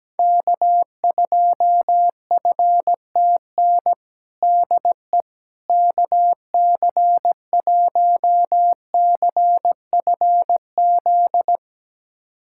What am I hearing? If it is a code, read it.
Morse code, heard as a keyed tone, about 17 wpm, 700 Hz. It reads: K2FTN DE KC1CFZ
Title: USA Callsign DE Callsign 17wpm